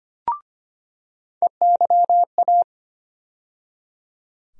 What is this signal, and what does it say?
Morse code, heard as a keyed tone, about 25 wpm, 700 Hz. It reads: EYA
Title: USA Call Sign Postfixes - 3 Letters 25wpm